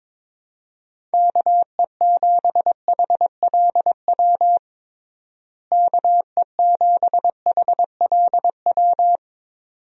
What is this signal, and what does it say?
Morse code, heard as a keyed tone, about 22 wpm, 700 Hz. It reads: KE7HLW KE7HLW
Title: USA Call Signs - 6 Characters - Repeated 2x - 22wpm